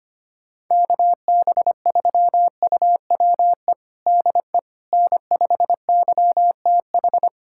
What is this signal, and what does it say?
Morse code, heard as a keyed tone, about 25 wpm, 700 Hz. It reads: KB3UWE DE N5YTH
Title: USA Callsign DE Callsign 25wpm